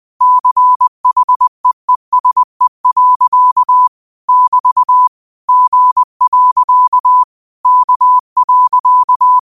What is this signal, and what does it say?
Morse code, heard as a keyed tone, about 20 wpm, 1000 Hz. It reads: CHEESE. <BT> G. K.
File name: qotdfiles / 20wpm